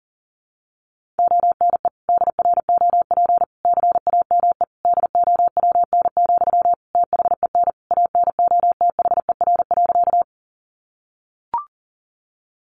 Morse code, heard as a keyed tone, about 40 wpm, 700 Hz. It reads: ONE DROP CAME DOWN, THEN ANOTHER.
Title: Sentences from Top 1400 Words 40wpm